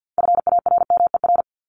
Morse code, heard as a keyed tone, about 48 wpm, 700 Hz. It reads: FARMER
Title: Top 1751-2000 Words 50wpm